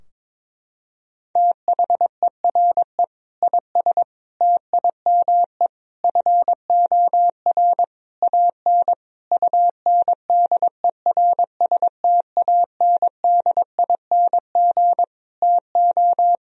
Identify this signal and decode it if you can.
Morse code, heard as a keyed tone, about 21 wpm, 700 Hz. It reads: THERE IS TIME FOR AN UNDERSTANDING TO
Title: Sentences from Top 1000 Words - Review Part 3 - 22wpm